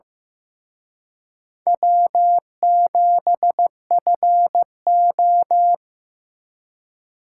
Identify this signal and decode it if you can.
Morse code, heard as a keyed tone, about 18 wpm, 700 Hz. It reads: W7FO
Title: Speed Racing - USA 1x2 Call Signs 15wpm